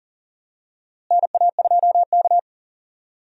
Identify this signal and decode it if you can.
Morse code, heard as a keyed tone, about 40 wpm, 700 Hz. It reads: NA2K
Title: USA Call Signs - 4 Characters - Repeated 2x - 40wpm